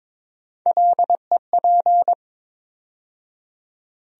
Morse code, heard as a keyed tone, about 22 wpm, 700 Hz. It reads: LEP
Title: USA Call Sign Postfixes - 3 Letters 22wpm